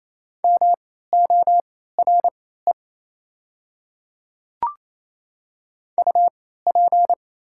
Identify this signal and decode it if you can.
Morse code, heard as a keyed tone, about 28 wpm, spaced as for 16 wpm, 700 Hz. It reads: MORE UP
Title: 3x Character Spacing - Top 100 Words - 28wpm